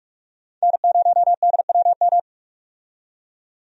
Morse code, heard as a keyed tone, about 45 wpm, 700 Hz. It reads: N0DWM
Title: USA Call Signs - 5 Characters - 45wpm